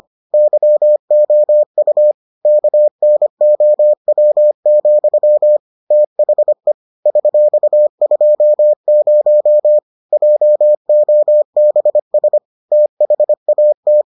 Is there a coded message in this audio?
Yes: Morse code from 0.3 to 14.0 s, heard as a keyed tone, about 25 wpm, 600 Hz. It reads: YOU KNOW, THE $20 JOBS THAT